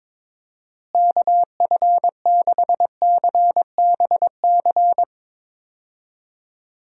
Morse code, heard as a keyed tone, about 22 wpm, 700 Hz. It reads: KF6CBC
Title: USA Call Signs - 6 Characters - Repeated 2x - 22wpm